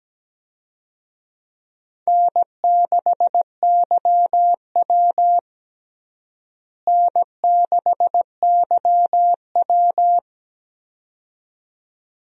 Morse code, heard as a keyed tone, about 17 wpm, 700 Hz. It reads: N6YW N6YW
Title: USA Call Signs - 4 Characters - Repeated 2x - 17wpm